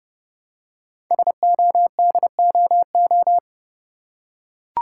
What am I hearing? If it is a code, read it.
Morse code, heard as a keyed tone, about 30 wpm, 700 Hz. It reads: SODOO
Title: Code Groups - 5 Letters - 30wpm